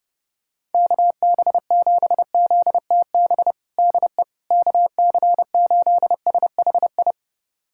Morse code, heard as a keyed tone, about 30 wpm, 700 Hz. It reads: KB7ZTB DE KC8SHI
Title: USA Callsign DE Callsign - 6 Characters - 30wpm